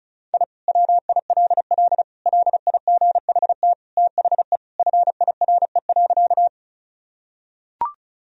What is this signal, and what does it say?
Morse code, heard as a keyed tone, about 36 wpm, 700 Hz. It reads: I WILL LIGHT THE FIRE.
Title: Sentences from Top 1000 Words - Review Part 1 - 35wpm